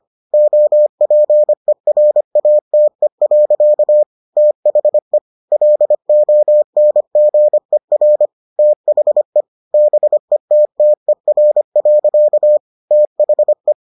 Morse code, heard as a keyed tone, about 25 wpm, 600 Hz. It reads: OPERATE. THE LONGER THE BETTER. THE